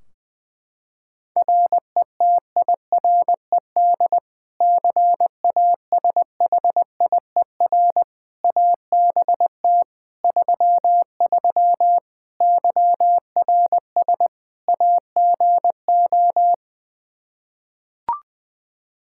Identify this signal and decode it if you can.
Morse code, heard as a keyed tone, about 20 wpm, 700 Hz. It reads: RETIRED CASHIER ABT 33 YRS AGO
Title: Practice QSO - Exchange 3 - More Info - 20wpm